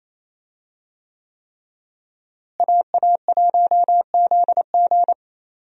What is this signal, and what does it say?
Morse code, heard as a keyed tone, about 28 wpm, 700 Hz. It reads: AA1ZG
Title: USA Call Signs - 5 Characters - Repeated 3x - 28wpm